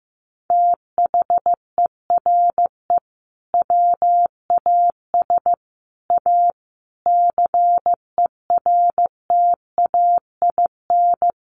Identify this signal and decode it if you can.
Morse code, heard as a keyed tone, about 15 wpm, 700 Hz. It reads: THERE WAS A CERTAIN
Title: Sentences from Top 1200 Words 15wpm